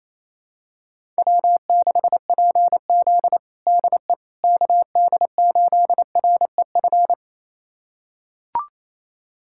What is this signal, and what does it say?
Morse code, heard as a keyed tone, about 28 wpm, 700 Hz. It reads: W6PZ DE KD8REF
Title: USA Callsign DE Callsign 28wpm